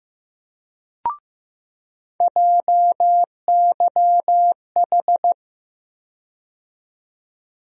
Morse code, heard as a keyed tone, about 15 wpm, 700 Hz. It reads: JYH
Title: USA Call Sign Postfixes - 3 Letters 15wpm